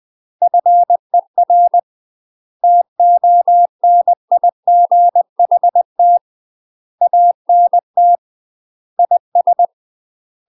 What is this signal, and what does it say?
Morse code, heard as a keyed tone, about 20 wpm, 700 Hz. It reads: FER TONIGHT ANT IS
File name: UK MASTER QSOs TRAINING / 20WPM / UK MASTER QSOs 0024